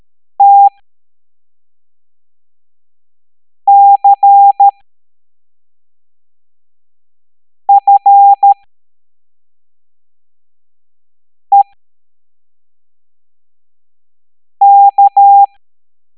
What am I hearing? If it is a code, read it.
Morse code, heard as a keyed tone, about 13 wpm, spaced as for 3 wpm, 800 Hz. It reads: TCFEK